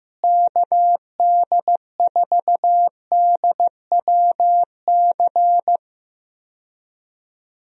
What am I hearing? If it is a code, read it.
Morse code, heard as a keyed tone, about 15 wpm, 700 Hz. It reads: KD4DWC
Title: USA Call Signs - 6 Characters - Repeated 3x - 15wpm